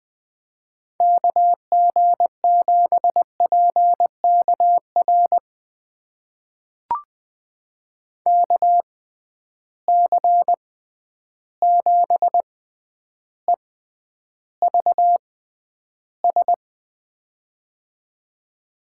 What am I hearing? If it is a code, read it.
Morse code, heard as a keyed tone, about 20 wpm, spaced as for 10 wpm, 700 Hz. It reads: KG7PKR KC7EVS
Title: USA Call Signs - 6 Characters - 6x Character Spacing - 20wpm